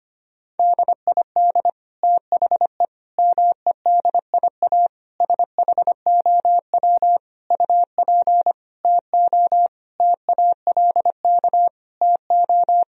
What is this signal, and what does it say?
Morse code, heard as a keyed tone, about 25 wpm, 700 Hz. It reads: DID THE MEDIA SHOW UP TO TALK TO
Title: Sentences from Top 1000 Words - Review Part 4 - 25wpm